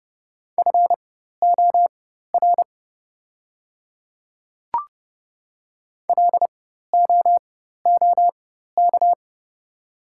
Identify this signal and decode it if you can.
Morse code, heard as a keyed tone, about 30 wpm, spaced as for 14 wpm, 700 Hz. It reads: FOR LOOK
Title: 4x Character Spacing - Top 100 Words - 30wpm